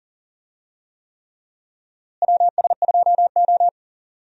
Morse code, heard as a keyed tone, about 40 wpm, 700 Hz. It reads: WS2O
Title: USA Call Signs - 4 Characters - Repeated 3x - 40wpm